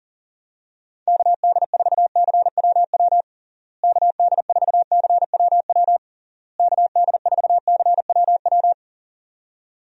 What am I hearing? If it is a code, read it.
Morse code, heard as a keyed tone, about 40 wpm, 700 Hz. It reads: KD4CWW KD4CWW KD4CWW
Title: USA Call Signs - 6 Characters - Repeated 3x - 40wpm